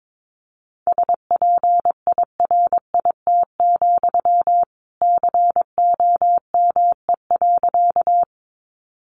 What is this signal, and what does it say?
Morse code, heard as a keyed tone, about 22 wpm, 700 Hz. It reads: SPIRIT, COME.
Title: Sentences from Top 1400 Words 22wpm